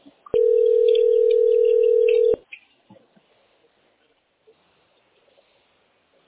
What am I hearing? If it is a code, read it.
Ringback tone (North America), 440 + 480 Hz.